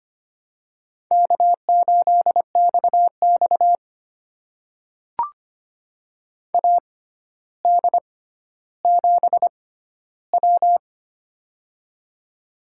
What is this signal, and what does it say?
Morse code, heard as a keyed tone, about 25 wpm, spaced as for 13 wpm, 700 Hz. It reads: K8XX AD7W
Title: USA Call Signs - 4 Characters - 6x Character Spacing - 25wpm